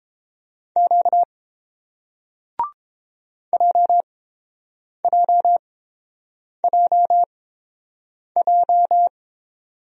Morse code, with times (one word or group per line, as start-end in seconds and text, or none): 0.8-1.2 Q
3.5-9.1 JJJJ